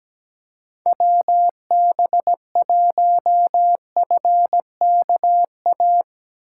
Morse code, heard as a keyed tone, about 17 wpm, 700 Hz. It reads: WB1FKA
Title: Speed Racing - USA Call Signs 17wpm